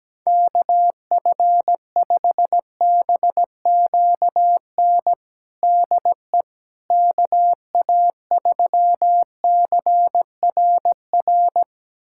Morse code, heard as a keyed tone, about 17 wpm, 700 Hz. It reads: KF5BQN DE KA3CRR
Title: USA Callsign DE Callsign - 6 Characters - 17wpm